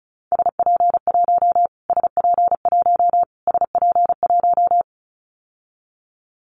Morse code, heard as a keyed tone, about 34 wpm, 700 Hz. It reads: SP1 SP1 SP1
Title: Callsign Prefixes for Top 25 DXCC - Repeated 3x - 35wpm